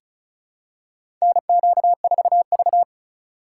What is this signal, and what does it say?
Morse code, heard as a keyed tone, about 36 wpm, 700 Hz. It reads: NQ4V
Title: USA Call Signs - 4 Characters - 3x Character Spacing - 35wpm